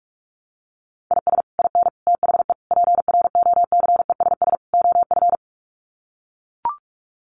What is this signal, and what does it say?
Morse code, heard as a keyed tone, about 44 wpm, 700 Hz. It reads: IS IN THE PROCESS OF
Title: Sets of 5 Words Encore2 45wpm